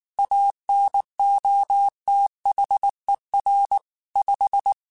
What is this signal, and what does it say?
Morse code, heard as a keyed tone, about 19 wpm, 800 Hz. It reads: ANOTHER 5
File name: QSOs / QSOs-mp3s / WA7QOX 1@18